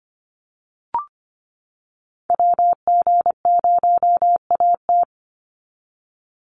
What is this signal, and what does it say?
Morse code, heard as a keyed tone, about 25 wpm, 700 Hz. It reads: WG0AT
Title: NA SOTA Activator Callsigns 25wpm